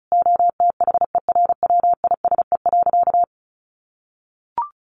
Morse code, heard as a keyed tone, about 34 wpm, 700 Hz. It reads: OTHERWISE.
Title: Sentences from Top 1500 Words 35wpm